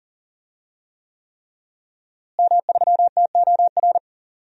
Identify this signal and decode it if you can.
Morse code, heard as a keyed tone, about 40 wpm, 700 Hz. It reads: M3TOR
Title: DXCC-100 Call Signs - 5 Characters - Repeated 3x - 40wpm